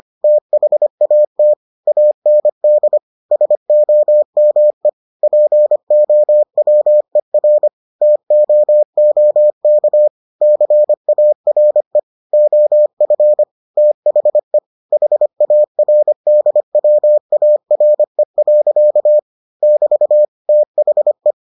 Morse code, heard as a keyed tone, about 25 wpm, 600 Hz. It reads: THAT AND SOME POWER TOOK CARE OF THE HARDWARE. <BT> THE